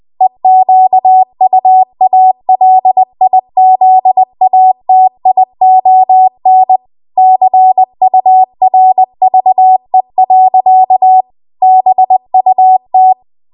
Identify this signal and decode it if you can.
Morse code, heard as a keyed tone, about 20 wpm, 750 Hz. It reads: EQUALIZATION CURVE. BUT